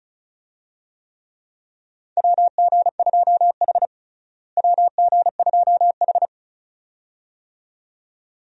Morse code, heard as a keyed tone, about 36 wpm, 700 Hz. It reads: WG2H WG2H
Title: USA Call Signs - 4 Characters - Repeated 2x - 35wpm